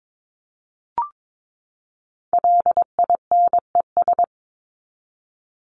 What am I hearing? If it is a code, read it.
Morse code, heard as a keyed tone, about 22 wpm, 700 Hz. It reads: LINES
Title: ES:S Suffix - Top 100 Words with Base Words 22wpm